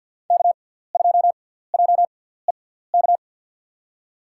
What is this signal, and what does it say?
Morse code, heard as a keyed tone, about 50 wpm, spaced as for 17 wpm, 700 Hz. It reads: K2JEK
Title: USA Call Signs - 5 Characters - 6x Character Spacing - 50wpm